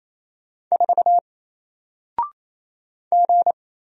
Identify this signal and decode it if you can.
Morse code, heard as a keyed tone, about 27 wpm, 700 Hz. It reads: 4 G
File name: Single-Letter-Number-28wpm